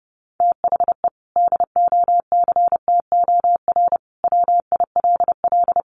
Morse code, heard as a keyed tone, about 30 wpm, 700 Hz. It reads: THE DOCTOR WILL
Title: Sentences from Top 1500 Words 30wpm